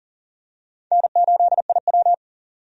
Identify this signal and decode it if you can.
Morse code, heard as a keyed tone, about 39 wpm, 700 Hz. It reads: N8IW
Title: USA Call Signs - 4 Characters - 40wpm